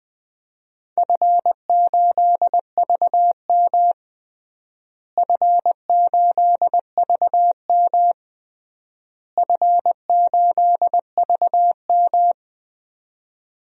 Morse code, heard as a keyed tone, about 19 wpm, 700 Hz. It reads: F8VM F8VM F8VM
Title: DXCC-100 Call Signs - 4 Characters - Repeated 3x - 20wpm